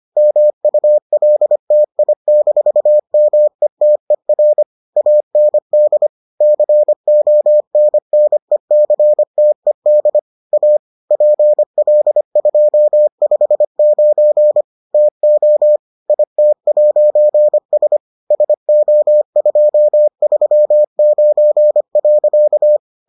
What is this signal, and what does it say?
Morse code, heard as a keyed tone, about 25 wpm, 600 Hz. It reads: MULTI-METER AND CONNECTED A PL259 TO IT'S SO239.